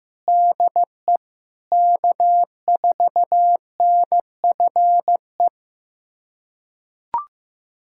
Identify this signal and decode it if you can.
Morse code, heard as a keyed tone, about 15 wpm, 700 Hz. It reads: DE K4NFE
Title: USA Callsign DE Callsign 15wpm